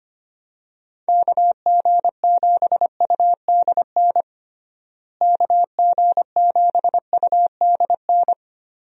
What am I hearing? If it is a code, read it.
Morse code, heard as a keyed tone, about 25 wpm, 700 Hz. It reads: KG7UDN KG7UDN
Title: USA Call Signs - 6 Characters - Repeated 3x - 25wpm